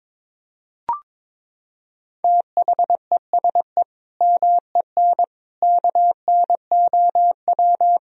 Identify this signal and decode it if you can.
Morse code, heard as a keyed tone, about 22 wpm, 700 Hz. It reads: THESE MEN KNOW
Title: Sentences from Top 1000 Words - Review Part 1 - 22wpm